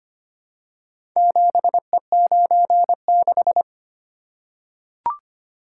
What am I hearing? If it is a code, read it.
Morse code, heard as a keyed tone, about 25 wpm, 700 Hz. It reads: 7E96
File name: practice / 25wpm / Four Letter-Number Combinations - 25wpm